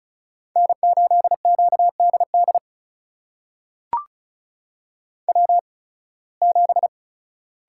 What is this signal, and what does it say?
Morse code, heard as a keyed tone, about 36 wpm, spaced as for 23 wpm, 700 Hz. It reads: N8QDD W7
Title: USA Call Signs - 8x Character Spacing - 35wpm